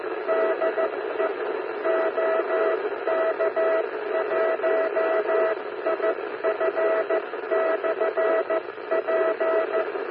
Morse code, heard as a keyed tone, about 14 wpm, 650 Hz, with radio noise. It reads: DE OK1IF/P
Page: CW na FM